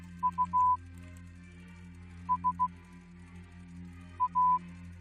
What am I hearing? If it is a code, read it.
Morse code, heard as a keyed tone, about 15 wpm, 1000 Hz. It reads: U S A